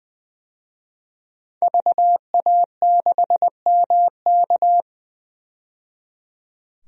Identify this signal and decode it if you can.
Morse code, heard as a keyed tone, about 20 wpm, 700 Hz. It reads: VA6MK
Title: DXCC-100 Call Signs - 5 Characters - 20wpm